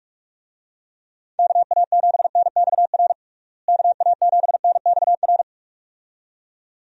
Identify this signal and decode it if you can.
Morse code, heard as a keyed tone, about 46 wpm, 700 Hz. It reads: KA7NXR KA7NXR
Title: USA Call Signs - 6 Characters - Repeated 2x - 45wpm